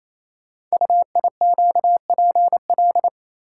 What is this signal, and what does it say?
Morse code, heard as a keyed tone, about 27 wpm, 700 Hz. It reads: UIQPL
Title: Code Groups - 5 Letters - 28wpm